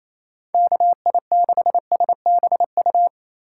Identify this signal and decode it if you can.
Morse code, heard as a keyed tone, about 34 wpm, 700 Hz. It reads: KI6SBU
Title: Speed Racing - USA 2x3 Call Signs 28wpm